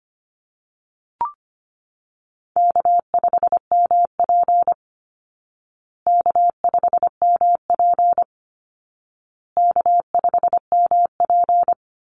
Morse code, heard as a keyed tone, about 25 wpm, 700 Hz. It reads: K5MP K5MP K5MP
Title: Call Signs Repeated 2x - 25wpm